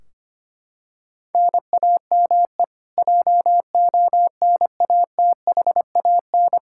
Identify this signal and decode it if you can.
Morse code, heard as a keyed tone, about 25 wpm, 700 Hz. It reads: NAME JONATHAN
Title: Practice QSO - Exchange 1 - Beginning - 25wpm